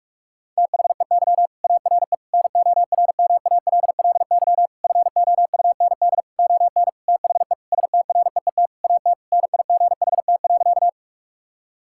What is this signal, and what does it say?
Morse code, heard as a keyed tone, about 46 wpm, 700 Hz. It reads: THEY ARE NORMALLY FOUND ON THE STREET AT NIGHT.
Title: Sentences from Top 1000 Words - Review Part 6 - 45wpm